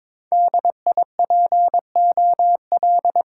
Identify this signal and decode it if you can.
Morse code, heard as a keyed tone, about 22 wpm, 700 Hz. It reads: D I P O L
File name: Top 90 QSO Elements - 6x Character Spacing - 22wpm